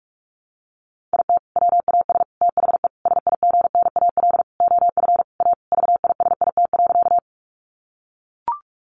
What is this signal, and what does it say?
Morse code, heard as a keyed tone, about 44 wpm, 700 Hz. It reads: IT WAS THE SIGNAL OF A VISIT.
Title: Sentences from Top 1400 Words 45wpm